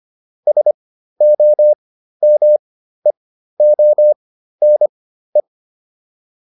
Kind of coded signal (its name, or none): Morse code